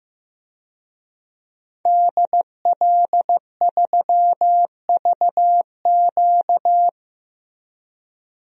Morse code, heard as a keyed tone, about 15 wpm, 700 Hz. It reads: DL3VQ
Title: DXCC-100 Call Signs - 5 Characters - Repeated 2x - 15wpm